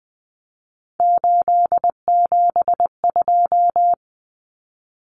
Morse code, heard as a keyed tone, about 20 wpm, 700 Hz. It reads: 872
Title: Common Cut Numbers - 3-Digit 20wpm